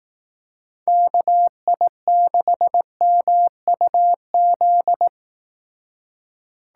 Morse code, heard as a keyed tone, about 18 wpm, 700 Hz. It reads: KI6MUZ
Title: Speed Racing - USA Call Signs 15wpm